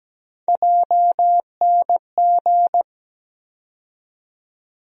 Morse code, heard as a keyed tone, about 17 wpm, 700 Hz. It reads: JNG
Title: USA Call Sign Postfixes - 3 Letters 17wpm